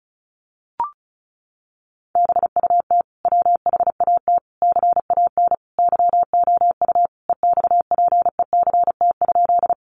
Morse code, heard as a keyed tone, about 34 wpm, 700 Hz. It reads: BUT WHAT CAN YOU EXPECT?
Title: Sentences from Top 1100 Words 35wpm